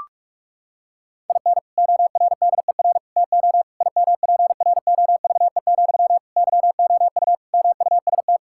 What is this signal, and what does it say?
Morse code, heard as a keyed tone, about 46 wpm, 700 Hz. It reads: IN ORDER TO IMPROVE, YOU MUST
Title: Sentences from Top 1000 Words - Review Part 5 - 45wpm